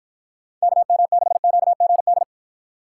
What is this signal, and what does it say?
Morse code, heard as a keyed tone, about 48 wpm, 700 Hz. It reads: KM6QGD